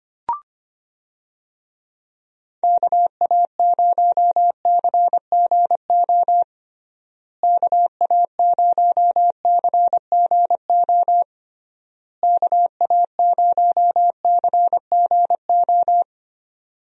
Morse code, heard as a keyed tone, about 25 wpm, 700 Hz. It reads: KA0CGO KA0CGO KA0CGO
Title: USA Call Signs - 6 Characters - Repeated 3x - 25wpm